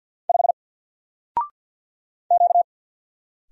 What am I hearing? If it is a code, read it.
Morse code, heard as a keyed tone, about 48 wpm, 700 Hz. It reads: 5 Q
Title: Single Letter-Number Alt 50wpm